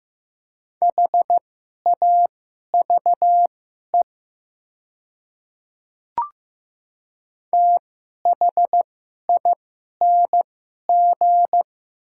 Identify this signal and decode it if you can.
Morse code, heard as a keyed tone, about 15 wpm, spaced as for 11 wpm, 700 Hz. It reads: HAVE THING
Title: 2x Character Spacing - Top 100 Words - 15wpm